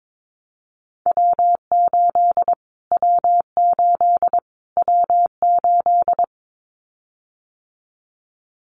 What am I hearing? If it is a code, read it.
Morse code, heard as a keyed tone, about 22 wpm, 700 Hz. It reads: W8 W8 W8
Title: Callsign Prefixes for Top 25 DXCC - Repeated 3x - 22wpm